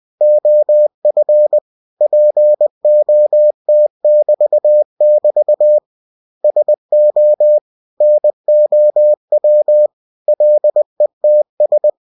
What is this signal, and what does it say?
Morse code, heard as a keyed tone, about 20 wpm, 600 Hz. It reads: OF POT<BT><BT> SO NOW LETS
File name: cw-books / winnie-the-pooh / winnie-20wpm-page-0104